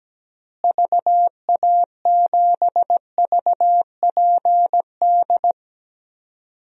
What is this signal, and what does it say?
Morse code, heard as a keyed tone, about 17 wpm, 700 Hz. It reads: VA7VPD
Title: DXCC-100 Call Signs - 6 Characters - Repeated 3x - 17wpm